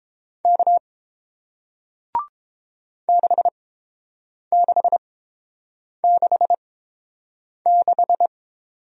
Morse code, heard as a keyed tone, about 30 wpm, spaced as for 7 wpm, 700 Hz. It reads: K 6666